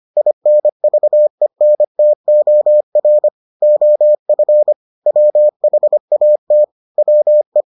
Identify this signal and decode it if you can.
Morse code, heard as a keyed tone, about 25 wpm, 600 Hz. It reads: INVENTOR OF WHAT WE